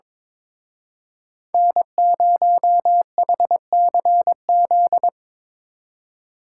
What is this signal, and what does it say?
Morse code, heard as a keyed tone, about 25 wpm, 700 Hz. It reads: N0HCZ
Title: Speed Racing - USA 1x3 Call Signs 22wpm